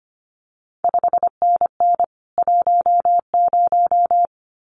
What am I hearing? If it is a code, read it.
Morse code, heard as a keyed tone, about 25 wpm, 700 Hz. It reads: 5NN 10
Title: CQ WW DX Contest Exchange Encore - 25wpm